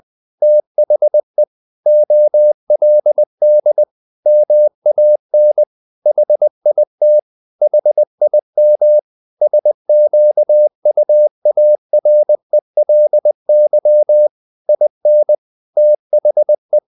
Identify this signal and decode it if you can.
Morse code, heard as a keyed tone, about 20 wpm, 600 Hz. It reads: THE OLD MAN HIT HIM SQUARELY IN THE
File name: the old man and the sea part 2 20wpm